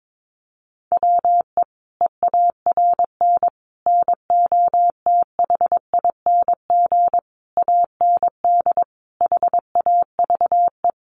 Morse code, heard as a keyed tone, about 22 wpm, 700 Hz. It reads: WE EARN NOTHING AND HAVE
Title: Sentences from Top 1500 Words 22wpm